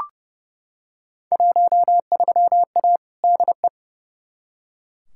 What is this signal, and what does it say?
Morse code, heard as a keyed tone, about 30 wpm, 700 Hz. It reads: 13A DE
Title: Field Day Exchanges Encore - 30wpm